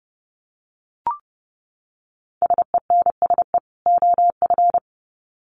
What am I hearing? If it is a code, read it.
Morse code, heard as a keyed tone, about 30 wpm, 700 Hz. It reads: SENSE OF
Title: Sets of 2 Words 30wpm